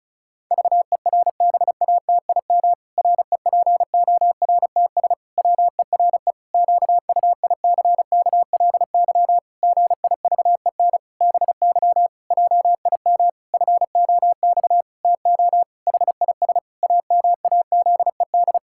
Morse code, heard as a keyed tone, about 36 wpm, 700 Hz. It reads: VERBATIM REPORTS WERE QUICKLY GIVEN BY JIM FOX TO HIS AMAZED
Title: Pangram Challenge v2 - 35wpm